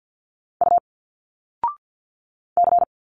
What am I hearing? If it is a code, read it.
Morse code, heard as a keyed tone, about 49 wpm, 700 Hz. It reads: U C
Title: Single Letters 50wpm